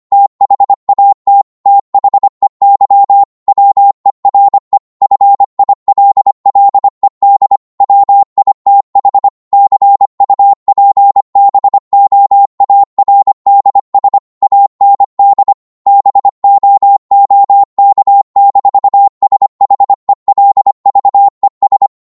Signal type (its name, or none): Morse code